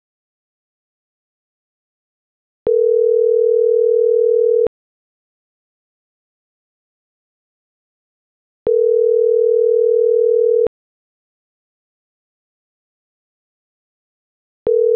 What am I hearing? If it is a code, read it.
Ringback tone (North America), 440 + 480 Hz.